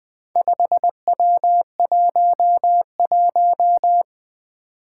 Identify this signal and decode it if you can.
Morse code, heard as a keyed tone, about 21 wpm, 700 Hz. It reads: 5W11
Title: Four Challenging Letter-Number Combinations 20wpm